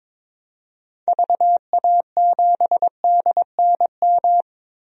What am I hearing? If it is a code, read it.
Morse code, heard as a keyed tone, about 22 wpm, 700 Hz. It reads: VA7DNM
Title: DXCC-100 Call Signs - 6 Characters - 22wpm